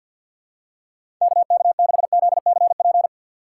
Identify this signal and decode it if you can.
Morse code, heard as a keyed tone, about 51 wpm, 690 Hz. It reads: KK6ZCP
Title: USA Call Signs - 6 Characters - 3x Character Spacing - 50wpm